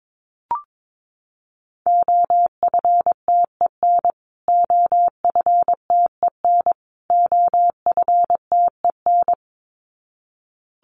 Morse code, heard as a keyed tone, about 22 wpm, 700 Hz. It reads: OFTEN OFTEN OFTEN
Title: Top 500 Words Repeated 22wpm